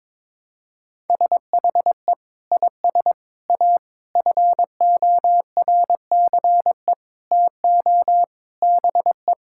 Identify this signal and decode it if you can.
Morse code, heard as a keyed tone, about 22 wpm, 700 Hz. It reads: SHE IS A FORCE TO BE
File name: practice / 22wpm / Sentences with 8-Word Phrases - 22wpm